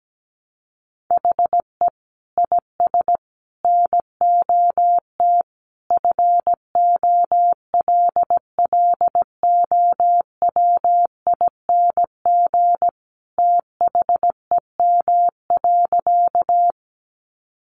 Morse code, heard as a keyed tone, about 17 wpm, 700 Hz. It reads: HE IS NOT FOLLOWING THEM.
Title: Sentences from Top 700 Words 17wpm